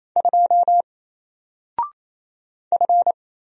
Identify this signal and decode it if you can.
Morse code, heard as a keyed tone, about 28 wpm, 700 Hz. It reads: 2 F
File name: Single-Letter-Number-28wpm